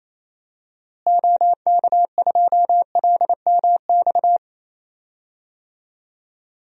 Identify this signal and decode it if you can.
Morse code, heard as a keyed tone, about 28 wpm, 700 Hz. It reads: OK2LMX
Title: DXCC-100 Call Signs - 6 Characters - Repeated 3x - 28wpm